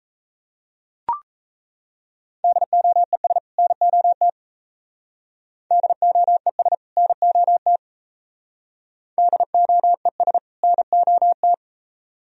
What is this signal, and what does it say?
Morse code, heard as a keyed tone, about 35 wpm, 700 Hz. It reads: DOES NOT DOES NOT DOES NOT
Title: Speed-Racing - Sets of 2 Words 28wpm